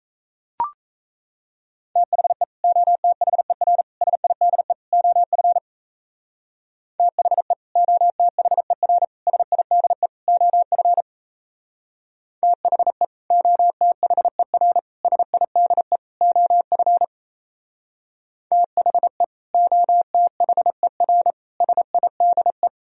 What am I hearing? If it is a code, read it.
Morse code, heard as a keyed tone, about 36 wpm, 700 Hz. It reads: THE OTHER SIDE OF THE OTHER SIDE OF THE OTHER SIDE OF THE OTHER SIDE